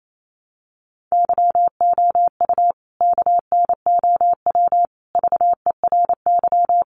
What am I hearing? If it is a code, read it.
Morse code, heard as a keyed tone, about 28 wpm, 700 Hz. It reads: YOU KNOW VERY
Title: A Sound Of Thunder 28wpm